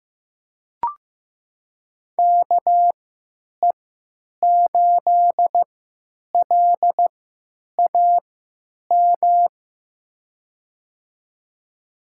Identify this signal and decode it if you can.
Morse code, heard as a keyed tone, about 15 wpm, spaced as for 9 wpm, 700 Hz. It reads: KE8LAM
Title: USA Call Signs - 6 Characters - 3x Character Spacing - 15wpm